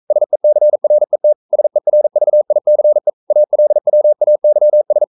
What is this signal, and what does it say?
Morse code, heard as a keyed tone, about 42 wpm, 600 Hz. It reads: SECRET SERVICE ALWAYS
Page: Romanen Never af Ken Follet på engelsk.